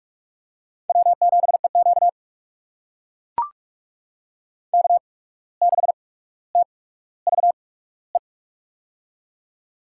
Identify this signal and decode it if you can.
Morse code, heard as a keyed tone, about 43 wpm, 690 Hz. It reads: W 7 E Q K 6 T V E
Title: USA Call Signs - 8x Character Spacing - 45wpm